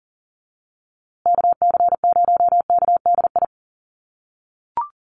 Morse code, heard as a keyed tone, about 40 wpm, 700 Hz. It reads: KC0KDI
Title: USA Call Signs Repeated 3x 40wpm